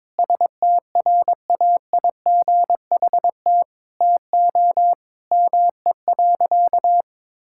Morse code, heard as a keyed tone, about 22 wpm, 700 Hz. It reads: STRAIGHT TO ME.
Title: Sentences from Top 1000 Words - Review Part 6 - 22wpm